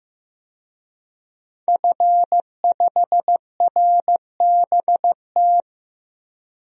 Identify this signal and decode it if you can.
Morse code, heard as a keyed tone, about 16 wpm, 700 Hz. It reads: F5RBT
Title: DXCC-100 Call Signs - 5 Characters - Repeated 3x - 15wpm